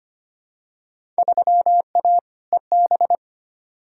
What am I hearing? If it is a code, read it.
Morse code, heard as a keyed tone, about 25 wpm, 700 Hz. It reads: 3A EB
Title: Field Day Exchanges Encore - 25wpm